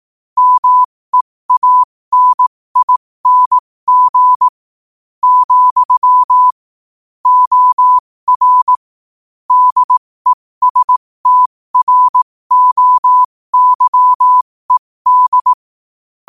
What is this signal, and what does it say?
Morse code, heard as a keyed tone, about 18 wpm, spaced as for 15 wpm, 1000 Hz. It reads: MEANING , OR DESTROYED
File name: qotdfiles / 15wpm